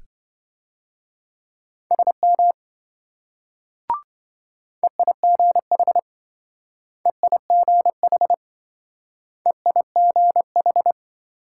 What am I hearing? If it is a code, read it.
Morse code, heard as a keyed tone, about 27 wpm, 700 Hz. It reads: SM EIGH EIGH EIGH